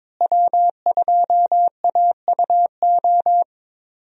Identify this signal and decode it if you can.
Morse code, heard as a keyed tone, about 22 wpm, 700 Hz. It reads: W2AUO
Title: USA Call Signs - 5 Characters - Repeated 3x - 22wpm